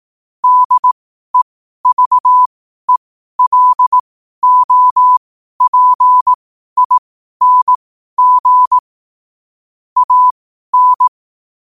Morse code, heard as a keyed tone, about 18 wpm, spaced as for 12 wpm, 1000 Hz. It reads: DEVELOPING AN
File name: qotdfiles / 12wpm